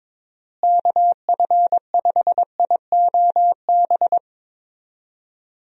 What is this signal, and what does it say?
Morse code, heard as a keyed tone, about 22 wpm, 700 Hz. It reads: KF5IOB
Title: USA Call Signs - 6 Characters - Repeated 2x - 22wpm